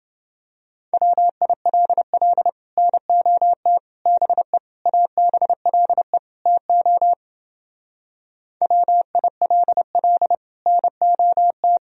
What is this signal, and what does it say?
Morse code, heard as a keyed tone, about 31 wpm, 700 Hz. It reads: WILL NOT BE ABLE TO WILL NOT